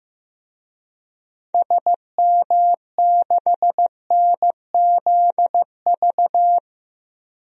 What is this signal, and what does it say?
Morse code, heard as a keyed tone, about 15 wpm, 700 Hz. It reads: SM6NZV
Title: DXCC-100 Call Signs - 6 Characters - 15wpm